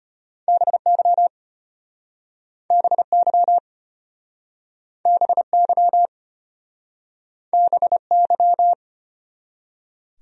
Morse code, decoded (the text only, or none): BY BY BY BY